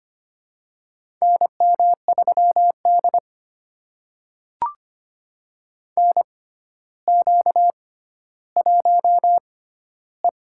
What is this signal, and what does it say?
Morse code, heard as a keyed tone, about 25 wpm, spaced as for 13 wpm, 700 Hz. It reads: NM3D NQ1E
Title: USA Call Signs - 4 Characters - 6x Character Spacing - 25wpm